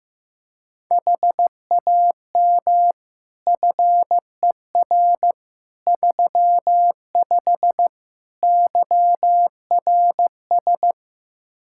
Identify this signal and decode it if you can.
Morse code, heard as a keyed tone, about 15 wpm, 700 Hz. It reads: HAM FER 35 YRS
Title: Practice QSO - Exchange 3 - More Info - 15wpm